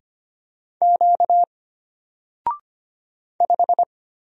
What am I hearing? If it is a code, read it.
Morse code, heard as a keyed tone, about 25 wpm, 700 Hz. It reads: Q 5
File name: Single-Letter-Number-25wpm